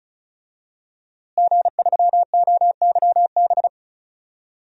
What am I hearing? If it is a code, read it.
Morse code, heard as a keyed tone, about 36 wpm, 700 Hz. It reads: G3OYB
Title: DXCC-100 Call Signs - 5 Characters - Repeated 2x - 35wpm